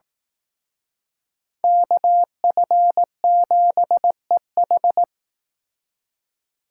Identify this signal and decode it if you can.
Morse code, heard as a keyed tone, about 18 wpm, 700 Hz. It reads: KF7EH
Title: Speed Racing - USA 2x2 Call Signs 15wpm